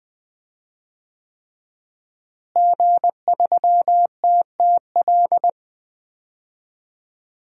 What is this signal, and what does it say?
Morse code, heard as a keyed tone, about 20 wpm, 700 Hz. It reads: G3TTL
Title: DXCC-100 Call Signs - 5 Characters - Repeated 3x - 20wpm